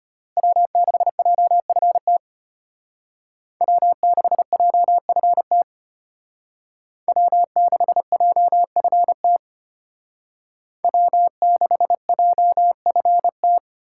Morse code, heard as a keyed tone, about 32 wpm, 700 Hz. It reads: W6JFT W6JFT W6JFT W6JFT